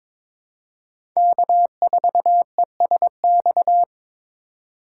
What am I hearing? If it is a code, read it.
Morse code, heard as a keyed tone, about 22 wpm, 700 Hz. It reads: K4ESX
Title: USA Call Signs - 5 Characters - 4x Character Spacing - 22wpm